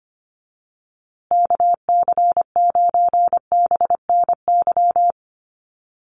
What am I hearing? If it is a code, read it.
Morse code, heard as a keyed tone, about 25 wpm, 700 Hz. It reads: KC9BNY
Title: USA Call Signs Repeated 2x 25wpm